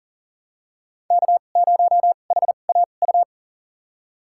Morse code, heard as a keyed tone, about 40 wpm, spaced as for 29 wpm, 700 Hz. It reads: K0HAU
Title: USA Call Signs - 5 Characters - 2x Character Spacing - 40wpm